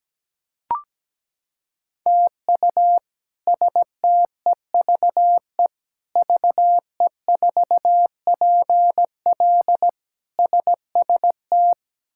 Morse code, heard as a keyed tone, about 17 wpm, 700 Hz. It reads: TU STEVE VE4PL SST
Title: K1USN SST QSO Practice - 17wpm